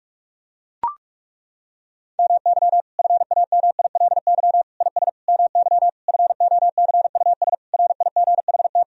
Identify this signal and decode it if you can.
Morse code, heard as a keyed tone, about 41 wpm, 700 Hz. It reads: MY FAMILY IS MY FOCUS RIGHT
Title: Sentences from Top 1000 Words - Review Part 4 - 45wpm